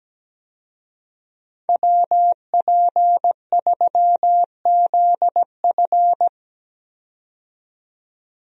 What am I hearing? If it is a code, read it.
Morse code, heard as a keyed tone, about 17 wpm, 700 Hz. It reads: WP3ZF
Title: USA Call Signs - 5 Characters - 17wpm